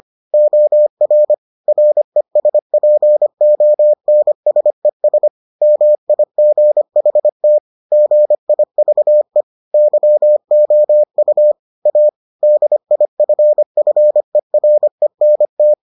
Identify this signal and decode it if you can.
Morse code, heard as a keyed tone, about 25 wpm, 600 Hz. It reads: OR RESPONSES MIGHT GIVE YOU A DIFFERENT